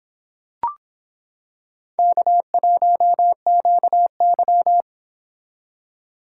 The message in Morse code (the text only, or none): K1QY